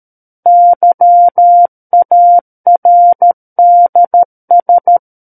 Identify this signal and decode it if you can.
Morse code, heard as a keyed tone, about 13 wpm, 700 Hz. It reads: YARDS
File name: Book One / 04-Chapter 4 - 13 WPM